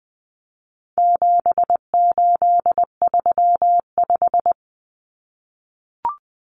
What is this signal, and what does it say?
Morse code, heard as a keyed tone, about 20 wpm, 700 Hz. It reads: 7835
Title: Numbers - 4-Digit 20wpm